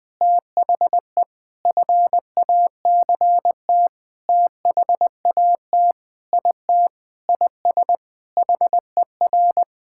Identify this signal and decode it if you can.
Morse code, heard as a keyed tone, about 20 wpm, 700 Hz. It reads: THE FACT THAT IT IS HER
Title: Sentences with 6-Word Phrases - Spoken Beginning - 20wpm